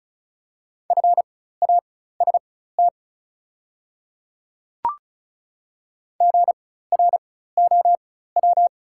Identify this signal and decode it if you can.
Morse code, heard as a keyed tone, about 36 wpm, spaced as for 16 wpm, 700 Hz. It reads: FAST GROW
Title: 4x Character Spacing - Top 500 Words - 35wpm